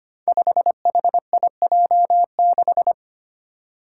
Morse code, heard as a keyed tone, about 25 wpm, 700 Hz. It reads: 5HIJ6
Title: Five Challenging Letter-Number Combinations 25wpm